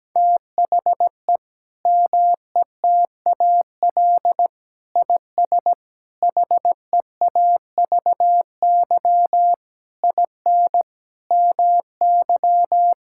Morse code, heard as a keyed tone, about 17 wpm, 700 Hz. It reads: THE METAL IS HEAVY IN MY
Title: Sentences from Top 1000 Words - Review Part 6 - 17wpm